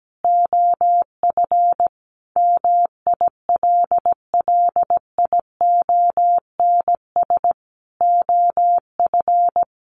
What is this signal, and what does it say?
Morse code, heard as a keyed tone, about 17 wpm, 700 Hz. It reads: OF MILLIONS OF
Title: Sets of 5 Words 17wpm